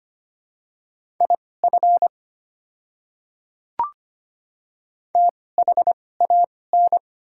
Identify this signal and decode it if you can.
Morse code, heard as a keyed tone, about 25 wpm, spaced as for 18 wpm, 690 Hz. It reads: IF THAN
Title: 2x Character Spacing - Top 100 Words - 25wpm